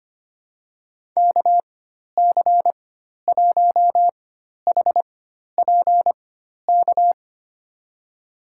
Morse code, heard as a keyed tone, about 25 wpm, spaced as for 12 wpm, 700 Hz. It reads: KC1HPK
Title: USA Call Signs - 6 Characters - 4x Character Spacing - 25wpm